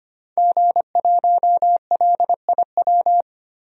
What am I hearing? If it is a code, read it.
Morse code, heard as a keyed tone, about 25 wpm, 690 Hz. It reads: G1LIW
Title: DXCC-100 Call Signs - 5 Characters - 25wpm